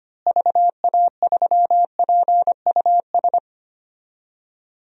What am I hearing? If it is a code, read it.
Morse code, heard as a keyed tone, about 25 wpm, 700 Hz. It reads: VA3PUS
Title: DXCC-100 Call Signs - 6 Characters - Repeated 2x - 25wpm